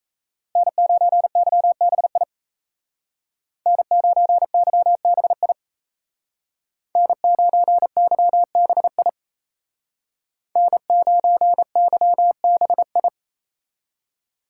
Morse code, heard as a keyed tone, about 36 wpm, 700 Hz. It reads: N9YBI N9YBI N9YBI N9YBI